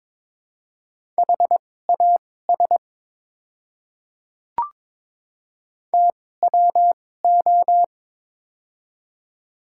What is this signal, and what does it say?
Morse code, heard as a keyed tone, about 21 wpm, spaced as for 16 wpm, 700 Hz. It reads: HAS TWO
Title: 2x Character Spacing - Top 100 Words - 22wpm